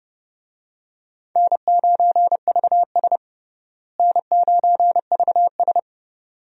Morse code, heard as a keyed tone, about 30 wpm, 700 Hz. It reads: N9VS N9VS
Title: USA Call Signs - 4 Characters - Repeated 2x - 30wpm